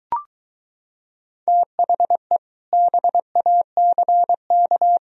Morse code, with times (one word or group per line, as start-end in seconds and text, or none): 1.5-2.4 THE
2.7-5.0 BACK